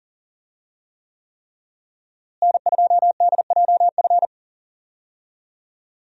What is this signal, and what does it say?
Morse code, heard as a keyed tone, about 39 wpm, 700 Hz. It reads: N2DJF
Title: USA Call Signs - 5 Characters - Repeated 3x - 40wpm